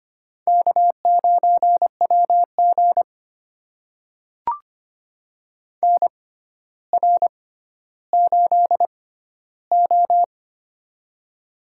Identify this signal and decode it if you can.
Morse code, heard as a keyed tone, about 25 wpm, spaced as for 13 wpm, 700 Hz. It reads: K9WG NR8O
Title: USA Call Signs - 4 Characters - 6x Character Spacing - 25wpm